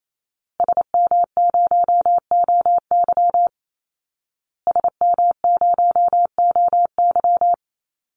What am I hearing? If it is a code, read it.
Morse code, heard as a keyed tone, about 28 wpm, 700 Hz. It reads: SM0OY SM0OY
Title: Call Signs Repeated 2x - 28wpm